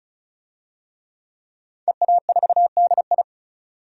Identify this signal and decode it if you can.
Morse code, heard as a keyed tone, about 36 wpm, 700 Hz. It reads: EA4DI
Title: DXCC-100 Call Signs - 5 Characters - 35wpm